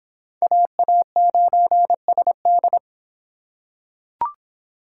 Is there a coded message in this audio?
Yes: Morse code from 0.4 to 2.8 s, heard as a keyed tone, about 25 wpm, 700 Hz. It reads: AA9SD